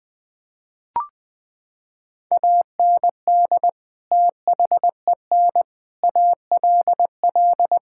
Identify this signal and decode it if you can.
Morse code, heard as a keyed tone, about 20 wpm, 700 Hz. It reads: AND THEN ALL
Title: Sets of 6 Words - 20wpm